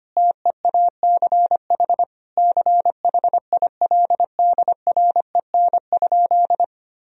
Morse code, heard as a keyed tone, about 25 wpm, 700 Hz. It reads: TEACH CHILDREN?
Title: Sentences from Top 1000 Words - Review Part 2 - 25wpm